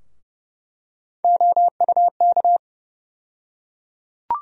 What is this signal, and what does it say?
Morse code, heard as a keyed tone, about 30 wpm, 700 Hz. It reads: OUK
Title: USA Call Sign Postfixes - 3 Letters 30wpm